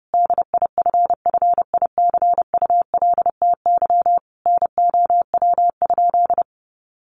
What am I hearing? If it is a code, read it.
Morse code, heard as a keyed tone, about 30 wpm, 700 Hz. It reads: DIFFICULTY NOW?
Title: Sentences from Top 1500 Words 30wpm